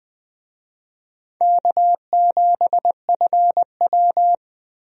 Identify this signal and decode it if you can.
Morse code, heard as a keyed tone, about 20 wpm, 700 Hz. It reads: K7FW
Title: USA Call Signs - 4 Characters - 20wpm